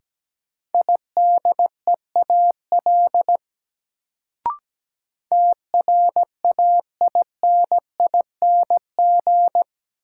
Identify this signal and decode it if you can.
Morse code, heard as a keyed tone, about 17 wpm, 700 Hz. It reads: IDEAL TRAINING
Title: Sentences from Top 1400 Words - New Word Drill Alt - 17wpm